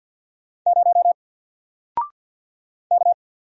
Morse code, heard as a keyed tone, about 50 wpm, 700 Hz. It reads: 0 K
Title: Single Character (Letter:Number:Punctuation:Prosign) 50wpm